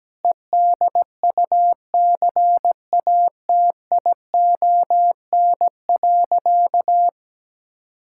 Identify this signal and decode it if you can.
Morse code, heard as a keyed tone, about 17 wpm, 700 Hz. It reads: EDUCATION.
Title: Sentences from Top 1000 Words - Review Part 6 - 17wpm